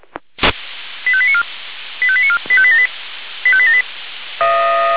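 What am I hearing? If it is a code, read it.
Five-tone selective call (ZVEI), heard as quick runs of five tones, tones of about 70 ms. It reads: 85894 85894 85878 85878
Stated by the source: ZVEI five-tone selective call received with an SDR.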